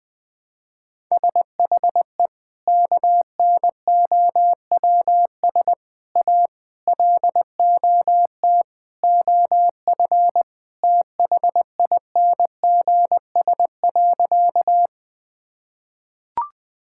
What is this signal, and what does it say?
Morse code, heard as a keyed tone, about 21 wpm, 700 Hz. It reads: SHE KNOWS A LOT OF THINGS.
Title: Sentences with 4-Word Phrases - 20wpm